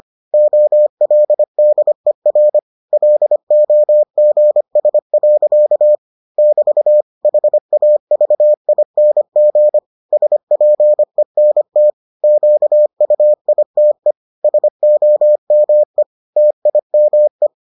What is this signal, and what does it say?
Morse code, heard as a keyed tone, about 25 wpm, 600 Hz. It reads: OLDER LOGS. <BT> HAVING SPENT QUITE SOME TIME